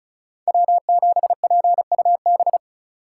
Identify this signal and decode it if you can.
Morse code, heard as a keyed tone, about 46 wpm, 700 Hz. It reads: W7PUB